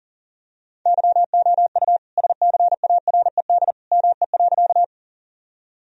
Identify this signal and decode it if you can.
Morse code, heard as a keyed tone, about 40 wpm, 700 Hz. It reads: YOU SCARED ME.
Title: Sentences from Top 1000 Words - Review Part 4 - 40wpm